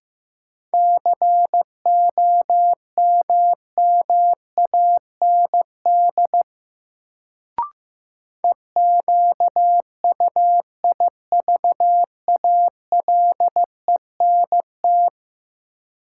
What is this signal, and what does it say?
Morse code, heard as a keyed tone, about 15 wpm, 700 Hz. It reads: COMMAND EQUIVALENT
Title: Sentences from Top 1600 Words - New Word Drill - 15wpm